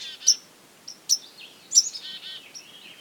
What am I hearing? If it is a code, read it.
There is no code.